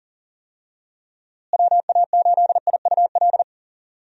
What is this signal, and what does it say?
Morse code, heard as a keyed tone, about 40 wpm, 700 Hz. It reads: WA8IUL
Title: USA Call Signs - 6 Characters - Repeated 3x - 40wpm